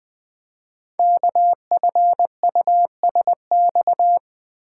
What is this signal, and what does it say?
Morse code, heard as a keyed tone, about 20 wpm, 700 Hz. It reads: KFUSX
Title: Code Groups - 5 Letters - 20wpm